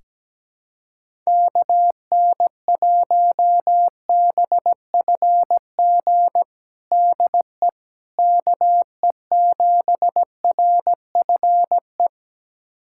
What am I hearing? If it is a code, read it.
Morse code, heard as a keyed tone, about 17 wpm, 700 Hz. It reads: KN1BFG DE KE7RFE
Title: USA Callsign DE Callsign - 6 Characters - 17wpm